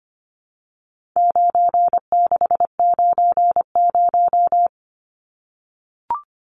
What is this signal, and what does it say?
Morse code, heard as a keyed tone, about 25 wpm, 700 Hz. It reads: 9690
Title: Numbers - 4-Digit 25wpm